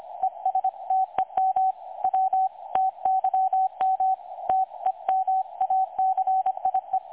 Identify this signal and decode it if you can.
Morse code, heard as a keyed tone, about 25 wpm, 760 Hz. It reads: ESTEM W TYM TEMACIE
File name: dlazaawansowanych / nagraniazpasma / websdr recording 3565.0kHz